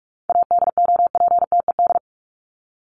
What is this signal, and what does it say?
Morse code, heard as a keyed tone, about 44 wpm, 700 Hz. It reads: ADOPTED
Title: ED Suffix - Top 100 Words with Base Words 45wpm